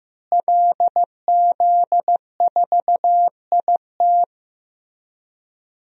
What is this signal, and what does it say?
Morse code, heard as a keyed tone, about 15 wpm, 700 Hz. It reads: LZ4IT
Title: DXCC-100 Call Signs - 5 Characters - Repeated 2x - 15wpm